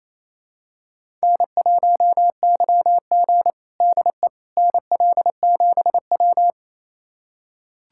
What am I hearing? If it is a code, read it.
Morse code, heard as a keyed tone, about 28 wpm, 700 Hz. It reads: N1YG DE NL7W
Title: USA Callsign DE Callsign - 4 Characters - 28wpm